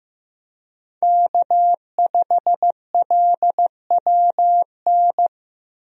Morse code, heard as a keyed tone, about 15 wpm, 700 Hz. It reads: K5LWN
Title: USA Call Signs - 5 Characters - Repeated 3x - 15wpm